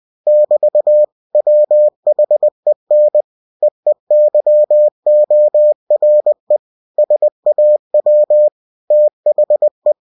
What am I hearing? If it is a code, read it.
Morse code, heard as a keyed tone, about 20 wpm, 600 Hz. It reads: <BT> WHEN EEYORE SAW THE
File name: cw-books / winnie-the-pooh / winnie-20wpm-page-0057